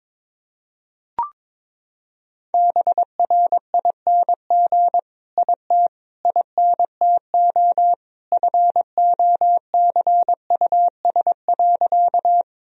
Morse code, heard as a keyed tone, about 22 wpm, 700 Hz. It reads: BRING IT INTO FOCUS.
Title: Sentences from Top 1000 Words - Review Part 4 - 22wpm